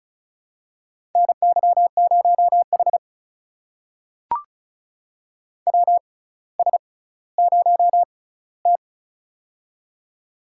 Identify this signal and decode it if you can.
Morse code, heard as a keyed tone, about 36 wpm, spaced as for 18 wpm, 700 Hz. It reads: NY0H WS0T
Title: USA Call Signs - 4 Characters - 6x Character Spacing - 35wpm